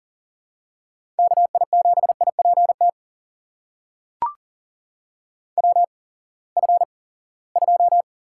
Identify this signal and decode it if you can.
Morse code, heard as a keyed tone, about 40 wpm, spaced as for 23 wpm, 700 Hz. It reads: KI7IPT WF2
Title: USA Call Signs - 8x Character Spacing - 40wpm